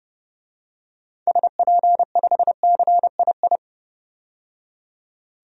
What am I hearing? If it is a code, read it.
Morse code, heard as a keyed tone, about 31 wpm, 700 Hz. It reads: SP5CII
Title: DXCC-100 Call Signs - 6 Characters - 30wpm